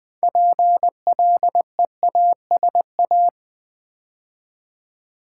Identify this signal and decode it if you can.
Morse code, heard as a keyed tone, about 20 wpm, 700 Hz. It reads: PLEASA
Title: Build A Word - 2 Letters At A Time - 8-Letter Words - 20wpm